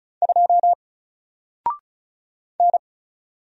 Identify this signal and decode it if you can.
Morse code, heard as a keyed tone, about 36 wpm, 700 Hz. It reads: 2 N
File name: Single-Letter-Number-35wpm